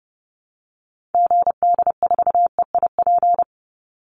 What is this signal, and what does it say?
Morse code, heard as a keyed tone, about 30 wpm, 700 Hz. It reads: GD4EIP
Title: Call Signs Repeated 3x - 30wpm